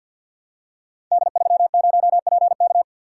Morse code, heard as a keyed tone, about 48 wpm, 700 Hz. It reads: D30PK
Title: Code Groups - 5 Letter:Numbers - 50wpm